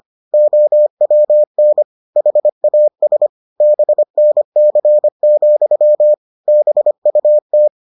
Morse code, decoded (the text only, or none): OWN HAS BNC, BUT